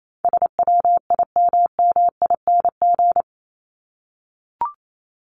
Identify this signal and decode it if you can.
Morse code, heard as a keyed tone, about 28 wpm, 700 Hz. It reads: SWIMMING
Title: Top 1501-1750 Words 28wpm